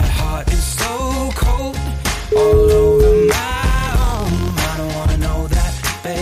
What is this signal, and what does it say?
Ringback tone (Europe, 425 Hz), with noise.